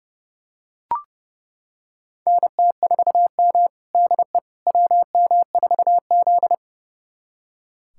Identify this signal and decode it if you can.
Morse code, heard as a keyed tone, about 30 wpm, 700 Hz. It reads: NT4M DE WM4Z
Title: USA Callsign DE Callsign - 4 Characters - 30wpm